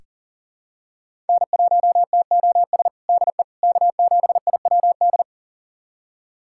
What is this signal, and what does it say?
Morse code, heard as a keyed tone, about 40 wpm, 700 Hz. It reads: N1TOS DE K7IWD
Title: USA Callsign DE Callsign - 5 Characters - 40wpm